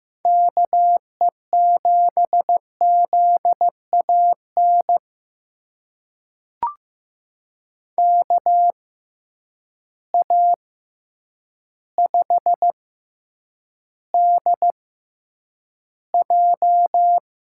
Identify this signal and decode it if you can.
Morse code, heard as a keyed tone, about 15 wpm, spaced as for 8 wpm, 700 Hz. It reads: KE7ZAN KA5DJ
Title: USA Call Signs - 6x Character Spacing - 15wpm